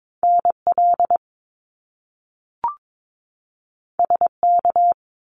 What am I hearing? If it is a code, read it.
Morse code, heard as a keyed tone, about 22 wpm, 700 Hz. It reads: NL SK
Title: Canadian Provinces and Territories 22wpm